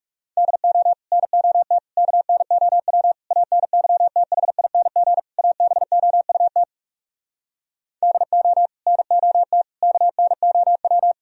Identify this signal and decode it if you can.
Morse code, heard as a keyed tone, about 40 wpm, 700 Hz. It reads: DO NOT KNOW ANYTHING ABOUT DO NOT KNOW